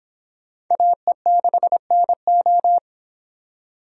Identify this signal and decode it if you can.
Morse code, heard as a keyed tone, about 26 wpm, 700 Hz. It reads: AE6NO